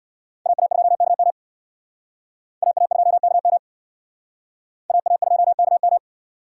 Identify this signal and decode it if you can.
Morse code, heard as a keyed tone, about 62 wpm, 700 Hz. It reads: AA2KM AA2KM AA2KM
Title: Speed Racing - USA Call Signs 50wpm